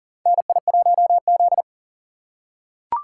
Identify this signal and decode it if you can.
Morse code, heard as a keyed tone, about 38 wpm, 700 Hz. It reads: NI1Z
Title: USA Call Signs - 4 Characters - 2x Character Spacing - 40wpm